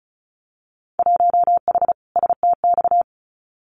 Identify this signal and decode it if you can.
Morse code, heard as a keyed tone, about 36 wpm, 700 Hz. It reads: 1H STX
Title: Winter Field Day Exchanges 35wpm